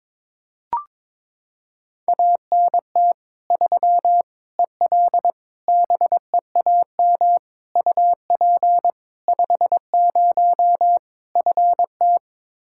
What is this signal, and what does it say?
Morse code, heard as a keyed tone, about 22 wpm, 700 Hz. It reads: ANT 3 EL BEAM UP 50 FT
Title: Practice QSO - Exchange 2 - Info - 22wpm